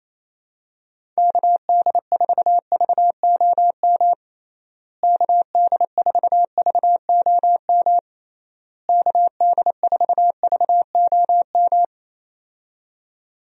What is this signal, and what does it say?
Morse code, heard as a keyed tone, about 27 wpm, 700 Hz. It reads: KD4VOM KD4VOM KD4VOM
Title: USA Call Signs - 6 Characters - Repeated 3x - 28wpm